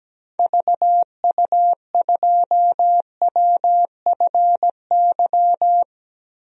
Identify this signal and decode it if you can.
Morse code, heard as a keyed tone, about 17 wpm, 700 Hz. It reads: VU2WFY
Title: DXCC-100 Call Signs - 6 Characters - 17wpm